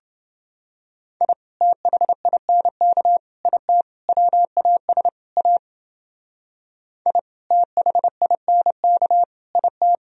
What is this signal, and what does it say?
Morse code, heard as a keyed tone, about 29 wpm, 700 Hz. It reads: I THINK IT WAS A I THINK IT